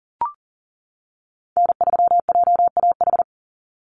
Morse code, heard as a keyed tone, about 40 wpm, 700 Hz. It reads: N3JAH
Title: USA Call Signs 40wpm